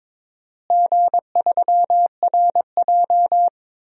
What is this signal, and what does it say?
Morse code, heard as a keyed tone, about 22 wpm, 700 Hz. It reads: G3RJ
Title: DXCC-100 Call Signs - 4 Characters - Repeated 2x - 22wpm